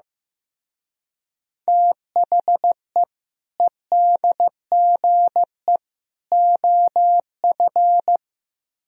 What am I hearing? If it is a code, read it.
Morse code, heard as a keyed tone, about 15 wpm, 700 Hz. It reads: THE EDGE OF
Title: Speed-Racing - Sets of 3 Words 15wpm